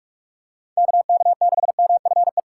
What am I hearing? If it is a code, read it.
Morse code, heard as a keyed tone, about 41 wpm, 700 Hz. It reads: KK6MFE
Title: USA Callsign DE Callsign - 6 Characters - 45wpm